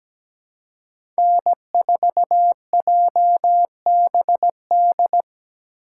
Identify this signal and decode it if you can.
Morse code, heard as a keyed tone, about 17 wpm, 700 Hz. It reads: N4JBD
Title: USA Call Signs - 5 Characters - 3x Character Spacing - 17wpm